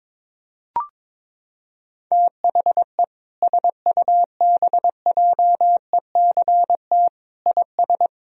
Morse code, heard as a keyed tone, about 22 wpm, 700 Hz. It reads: THE SUBJECT IS
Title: Sentences from Top 1000 Words - Review Part 4 - 22wpm